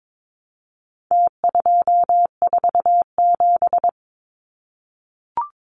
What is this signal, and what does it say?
Morse code, heard as a keyed tone, about 22 wpm, 700 Hz. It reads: T247
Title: Numbers - Contest Style - 4-Digit 22wpm